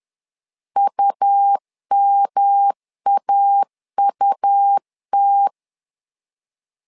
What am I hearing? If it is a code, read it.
Morse code, heard as a keyed tone, about 11 wpm, 800 Hz. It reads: UMAUT
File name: LEZIONI / 3 MWRO / 3 M 10wpm 1828